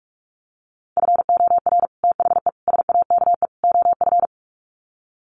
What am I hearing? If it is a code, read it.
Morse code, heard as a keyed tone, about 44 wpm, 700 Hz. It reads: FOR THE SAKE OF
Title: Sets of 4 Words 45wpm